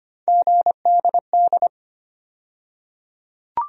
Morse code, heard as a keyed tone, about 25 wpm, 700 Hz. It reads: GDD
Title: USA Call Sign Postfixes - 3 Letters 25wpm